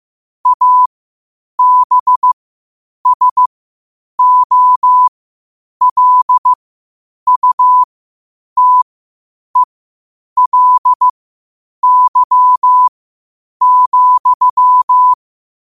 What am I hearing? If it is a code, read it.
Morse code, heard as a keyed tone, about 15 wpm, spaced as for 8 wpm, 1000 Hz. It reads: ABSOLUTELY,
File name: qotdfiles / 8wpm